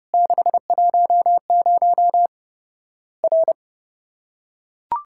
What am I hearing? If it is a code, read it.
Morse code, heard as a keyed tone, about 30 wpm, 700 Hz. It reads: 610 R
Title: Worked All Europe DX Contest - QTC Exchange - 30wpm